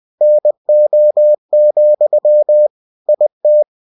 Morse code, heard as a keyed tone, about 20 wpm, 600 Hz. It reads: NO, IT
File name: cw-books / winnie-the-pooh / winnie-20wpm-page-0038